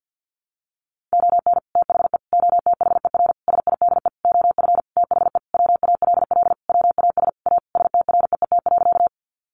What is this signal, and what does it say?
Morse code, heard as a keyed tone, about 50 wpm, 700 Hz. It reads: ON THE OTHER SIDE OF THE WALL WAS A STREET.
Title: Sentences from Top 1100 Words 50wpm